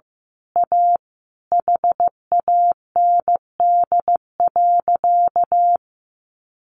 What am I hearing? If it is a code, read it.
Morse code, heard as a keyed tone, about 16 wpm, 700 Hz. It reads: A HAND.
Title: A Sound Of Thunder 15wpm